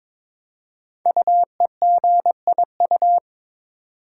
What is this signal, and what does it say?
Morse code, heard as a keyed tone, about 22 wpm, 700 Hz. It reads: UEGIU
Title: Code Groups - 5 Letters - 22wpm